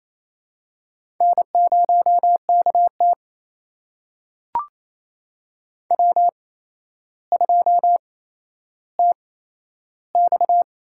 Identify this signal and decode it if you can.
Morse code, heard as a keyed tone, about 28 wpm, spaced as for 12 wpm, 700 Hz. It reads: N0KT W2TX
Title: USA Call Signs - 4 Characters - 8x Character Spacing - 28wpm